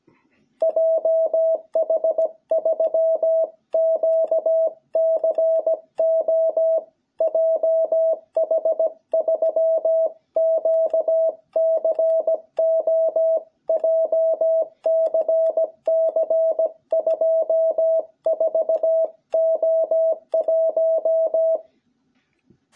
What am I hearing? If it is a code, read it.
Morse code, heard as a keyed tone, about 17 wpm, 660 Hz. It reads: JH3QCO JH3QCO JCC 24O1
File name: jh8cbh / 2021cw